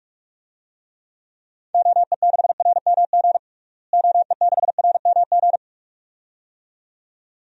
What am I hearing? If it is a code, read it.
Morse code, heard as a keyed tone, about 48 wpm, 700 Hz. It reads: OE6RMG OE6RMG
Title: DXCC-100 Call Signs - 6 Characters - Repeated 2x - 45wpm